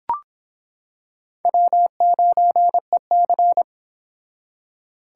Morse code, heard as a keyed tone, about 25 wpm, 700 Hz. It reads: W9EC